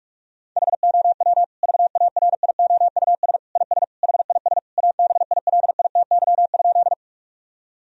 Morse code, heard as a keyed tone, about 46 wpm, 700 Hz. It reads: HOW VARIOUS IS HIS ABILITY?
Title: Sentences from Top 1000 Words - Review Part 4 - 45wpm